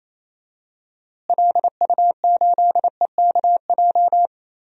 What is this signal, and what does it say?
Morse code, heard as a keyed tone, about 28 wpm, 700 Hz. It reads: LU8EKJ
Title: DXCC-100 Call Signs - 6 Characters - Repeated 3x - 28wpm